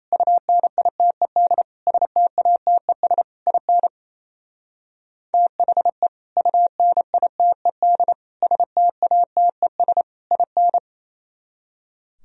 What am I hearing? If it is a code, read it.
Morse code, heard as a keyed tone, about 33 wpm, 700 Hz. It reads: UNITED STATES IN THE UNITED STATES IN
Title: Speed-Racing - Sets of 4 Words 28wpm